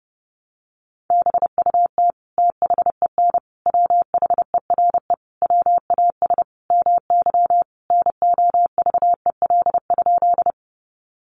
Morse code, heard as a keyed tone, about 30 wpm, 700 Hz. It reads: BUT THEN WHERE WAS MY NOVEL?
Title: Sentences from Top 1300 Words 30wpm